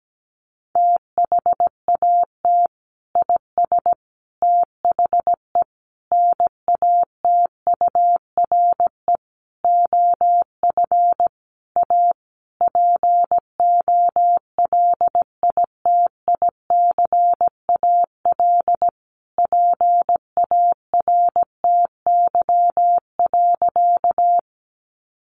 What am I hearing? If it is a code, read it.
Morse code, heard as a keyed tone, about 17 wpm, 700 Hz. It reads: THAT IS THE NATURE OF A POLITICAL PARTY.
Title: Sentences from Top 1400 Words 17wpm